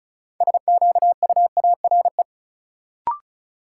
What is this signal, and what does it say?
Morse code, heard as a keyed tone, about 34 wpm, 700 Hz. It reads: SQUARE
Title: Sentences from Top 1000 Words - Review Part 7 - New Word Drill - 35wpm